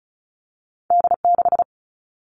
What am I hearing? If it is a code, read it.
Morse code, heard as a keyed tone, about 36 wpm, 700 Hz. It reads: D6
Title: All Cut Numbers - 2-Digit 35wpm